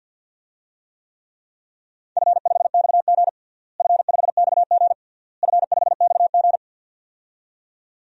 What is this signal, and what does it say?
Morse code, heard as a keyed tone, about 50 wpm, 700 Hz. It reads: F5XG F5XG F5XG
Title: DXCC-100 Call Signs - 4 Characters - Repeated 3x - 50wpm